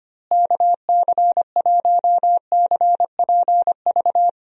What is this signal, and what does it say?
Morse code, heard as a keyed tone, about 25 wpm, 700 Hz. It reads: KC1CPV
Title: USA Call Signs - 6 Characters - Repeated 3x - 25wpm